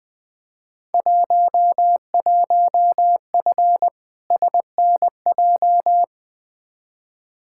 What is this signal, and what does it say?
Morse code, heard as a keyed tone, about 20 wpm, 700 Hz. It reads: 11F SNJ
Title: Field Day Exchanges Encore - 20wpm